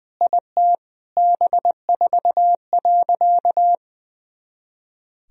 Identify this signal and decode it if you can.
Morse code, heard as a keyed tone, about 20 wpm, 700 Hz. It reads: IT B4.
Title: Sentences from Top 400 Words and Abbreviations 20wpm